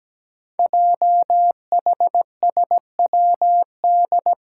Morse code, heard as a keyed tone, about 17 wpm, 700 Hz. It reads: JHSWD
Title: Code Groups - 5 Letters - 17wpm